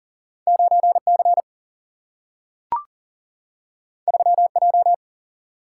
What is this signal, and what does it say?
Morse code, heard as a keyed tone, about 40 wpm, 700 Hz. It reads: 9C 3J
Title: Single-Letter Single-Number 40wpm